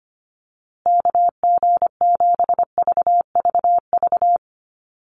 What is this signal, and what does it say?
Morse code, heard as a keyed tone, about 25 wpm, 700 Hz. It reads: KG7VVV
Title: USA Call Signs Repeated 3x 25wpm